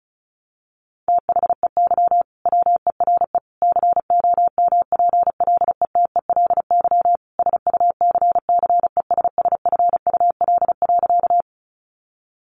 Morse code, heard as a keyed tone, about 34 wpm, 700 Hz. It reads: THEY WERE COMPLETELY SUCCESSFUL.
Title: Sentences from Top 1100 Words 35wpm